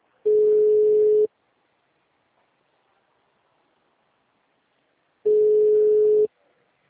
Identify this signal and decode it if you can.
Ringback tone (Europe, 425 Hz).